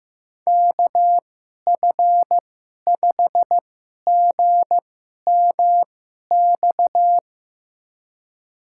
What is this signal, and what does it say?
Morse code, heard as a keyed tone, about 16 wpm, spaced as for 11 wpm, 700 Hz. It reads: KF5GMX
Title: USA Call Signs - 6 Characters - 2x Character Spacing - 15wpm